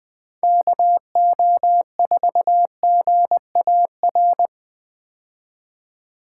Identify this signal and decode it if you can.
Morse code, heard as a keyed tone, about 20 wpm, 700 Hz. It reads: KO4GAR
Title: 2023 POTA Hunter Call Signs - 20wpm